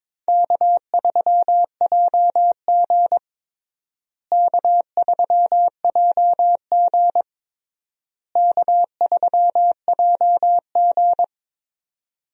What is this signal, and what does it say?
Morse code, heard as a keyed tone, about 22 wpm, 700 Hz. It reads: K3JG K3JG K3JG
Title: USA Call Signs - 4 Characters - Repeated 3x - 22wpm